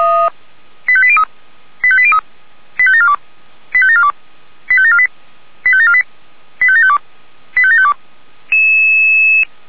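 Five-tone selective call (ZVEI), heard as quick runs of five tones, tones of about 70 ms. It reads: 85803 85803 85742 85742 85748 85748 85752 85752
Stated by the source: ZVEI five-tone selective call 85719 received with an SDR.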